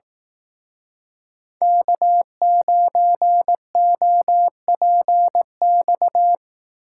Morse code, heard as a keyed tone, about 18 wpm, 700 Hz. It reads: K9OPX
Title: Speed Racing - USA 1x3 Call Signs 15wpm